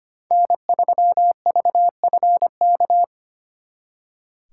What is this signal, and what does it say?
Morse code, heard as a keyed tone, about 33 wpm, 700 Hz. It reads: N3VFK